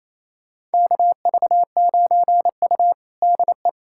Morse code, heard as a keyed tone, about 28 wpm, 700 Hz. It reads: KV9U DE
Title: USA Callsign DE Callsign - 4 Characters - 28wpm